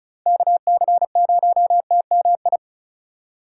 Morse code, heard as a keyed tone, about 36 wpm, 700 Hz. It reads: KC0TMI
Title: USA Call Signs - 6 Characters - 4x Character Spacing - 35wpm